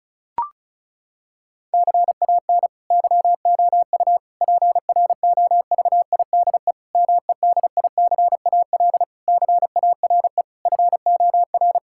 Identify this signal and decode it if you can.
Morse code, heard as a keyed tone, about 36 wpm, 700 Hz. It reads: CAN YOU PROVIDE MEDICAL CARE FOR
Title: Sentences from Top 1000 Words - Review Part 5 - 35wpm